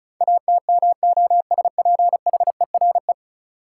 Morse code, heard as a keyed tone, about 36 wpm, 700 Hz. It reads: ATMOSPHERE
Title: Sentences from Top 1600 Words - New Word Drill Alt - 35wpm